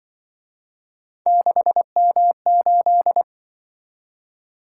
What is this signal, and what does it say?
Morse code, heard as a keyed tone, about 23 wpm, 700 Hz. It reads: 6M8
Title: Speed-Racing - Three Letter-Number Combinations - 20wpm